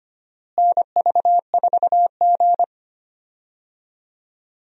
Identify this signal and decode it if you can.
Morse code, heard as a keyed tone, about 25 wpm, 700 Hz. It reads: NV4G
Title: USA Call Signs - 4 Characters - Repeated 3x - 25wpm